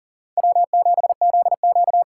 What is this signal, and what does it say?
Morse code, heard as a keyed tone, about 40 wpm, 700 Hz. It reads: W7ZQ
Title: USA Call Signs - 4 Characters - 6x Character Spacing - 40wpm